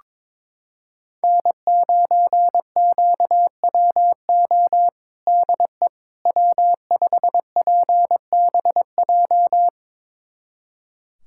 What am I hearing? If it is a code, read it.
Morse code, heard as a keyed tone, about 22 wpm, 700 Hz. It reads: N9QWO DE W5PBJ
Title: USA Callsign DE Callsign - 5 Characters - 22wpm